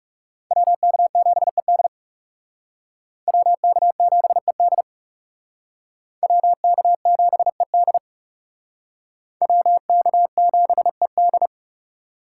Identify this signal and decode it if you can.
Morse code, heard as a keyed tone, about 39 wpm, 700 Hz. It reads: WK7ED WK7ED WK7ED WK7ED